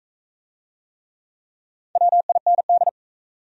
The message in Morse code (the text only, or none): WIND